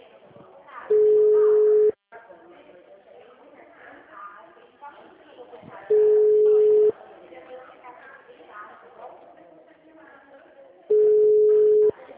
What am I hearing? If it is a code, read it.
Ringback tone (Europe, 425 Hz), with speech.